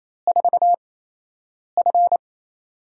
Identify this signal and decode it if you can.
Morse code, heard as a keyed tone, about 28 wpm, spaced as for 8 wpm, 700 Hz. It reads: 4F
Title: USA Call Signs - 8x Character Spacing - 28wpm